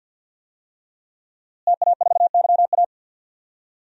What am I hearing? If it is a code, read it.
Morse code, heard as a keyed tone, about 50 wpm, 700 Hz. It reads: TA4YA
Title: DXCC-100 Call Signs - 5 Characters - Repeated 3x - 50wpm